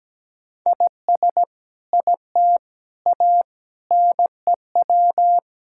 Morse code, heard as a keyed tone, about 17 wpm, 700 Hz. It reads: IS IT A NEW
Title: Sentences from Top 1000 Words - Review Part 5 - 17wpm